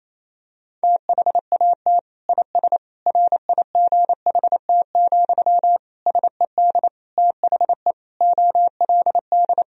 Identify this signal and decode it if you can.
Morse code, heard as a keyed tone, about 28 wpm, 700 Hz. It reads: THAT IS RIGHT, SED THE OLD
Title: Sentences from Top 200 Words and Abbreviations 28wpm